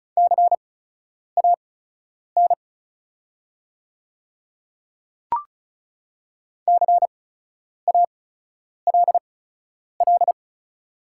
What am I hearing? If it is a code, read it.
Morse code, heard as a keyed tone, about 34 wpm, spaced as for 10 wpm, 700 Hz. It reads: CAN CALL
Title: 8x Character Spacing - Top 100 Words - 35wpm